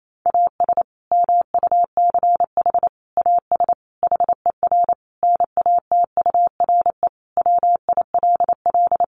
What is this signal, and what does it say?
Morse code, heard as a keyed tone, about 28 wpm, 700 Hz. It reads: AS MUCH AS HER NATURE WILL
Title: Sentences from Top 700 Words 28wpm